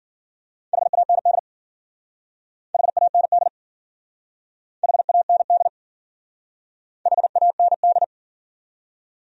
Morse code, heard as a keyed tone, about 51 wpm, 700 Hz. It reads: HAND HAND HAND HAND